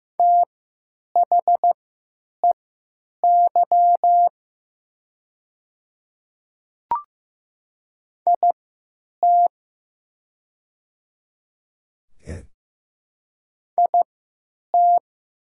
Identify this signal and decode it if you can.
Morse code, heard as a keyed tone, about 15 wpm, spaced as for 9 wpm, 690 Hz. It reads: THEY IT IT
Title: 3x Character Spacing - Top 100 Words - 15wpm